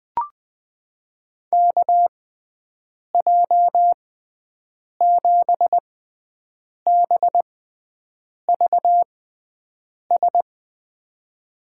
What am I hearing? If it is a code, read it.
Morse code, heard as a keyed tone, about 20 wpm, spaced as for 7 wpm, 700 Hz. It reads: KJ7BVS
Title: USA Call Signs - 6x Character Spacing - 20wpm